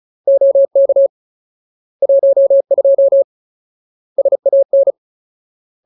Morse code, heard as a keyed tone, about 34 wpm, 550 Hz. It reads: OK 12 SAN
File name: Ultimate Word List / 35WPM Ultimate